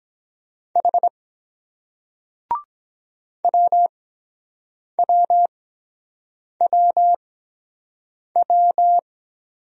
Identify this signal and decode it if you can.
Morse code, heard as a keyed tone, about 22 wpm, spaced as for 7 wpm, 700 Hz. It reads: H WWWW